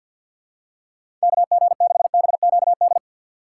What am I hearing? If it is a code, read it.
Morse code, heard as a keyed tone, about 52 wpm, 700 Hz. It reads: K G 6 B Q D
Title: USA Call Signs - 6x Character Spacing - 50wpm